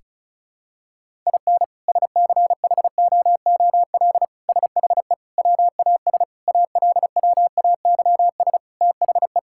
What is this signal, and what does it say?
Morse code, heard as a keyed tone, about 34 wpm, 700 Hz. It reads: IN SCHOOL SHE WAS ALWAYS THE
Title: Sentences from Top 1000 Words - Review Part 5 - 35wpm